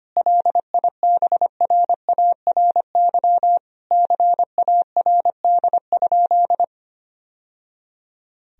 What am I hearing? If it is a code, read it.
Morse code, heard as a keyed tone, about 25 wpm, 700 Hz. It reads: LIBRARY CARD?
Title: Sentences from Top 1000 Words - Review Part 4 - 25wpm